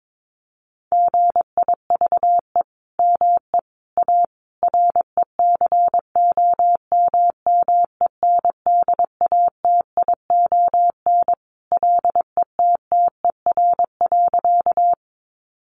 Morse code, heard as a keyed tone, about 22 wpm, 700 Hz. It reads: GIVE ME A RECOMMENDATION LETTER.
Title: Sentences from Top 1400 Words 22wpm